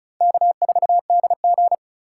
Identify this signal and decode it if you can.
Morse code, heard as a keyed tone, about 36 wpm, 700 Hz. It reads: K4DG
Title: USA Call Signs - 4 Characters - 4x Character Spacing - 35wpm